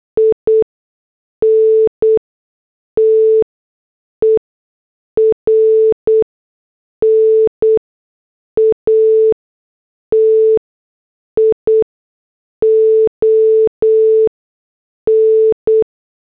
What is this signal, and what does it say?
Morse code, heard as a keyed tone, about 8 wpm, spaced as for 6 wpm, 440 Hz. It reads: INTERNATION